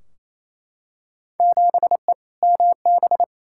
Morse code, heard as a keyed tone, about 28 wpm, 700 Hz. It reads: 7E MB
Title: Field Day Exchanges Encore - 28wpm